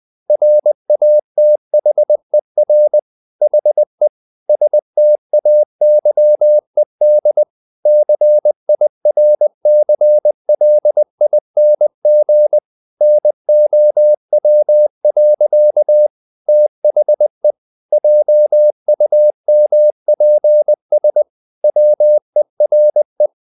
Morse code, heard as a keyed tone, about 20 wpm, 600 Hz. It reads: RATHER HE STAYED CIRCLING NOW. THE JUMPS WERE
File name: the old man and the sea part 2 20wpm